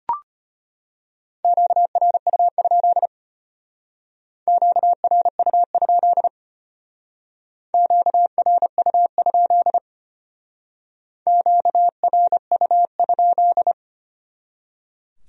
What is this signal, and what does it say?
Morse code, heard as a keyed tone, about 32 wpm, 700 Hz. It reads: QRU? QRU? QRU? QRU?